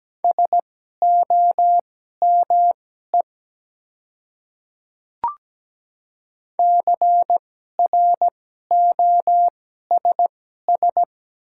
Morse code, heard as a keyed tone, about 17 wpm, spaced as for 12 wpm, 700 Hz. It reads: SOME CROSS
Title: 2x Character Spacing - Top 300 Words - 17wpm